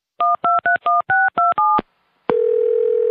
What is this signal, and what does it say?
DTMF, heard as telephone touch tones, tones of about 170 ms. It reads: 1231627